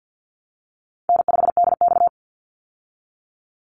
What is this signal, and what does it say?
Morse code, heard as a keyed tone, about 50 wpm, 700 Hz. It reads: N5DX
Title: Easy Call Signs Repeated 3x - 50wpm